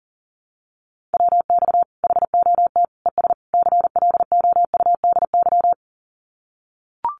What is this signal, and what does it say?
Morse code, heard as a keyed tone, about 40 wpm, 700 Hz. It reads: WX HOT ES CLOUDY
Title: Common QSO Phrases 40wpm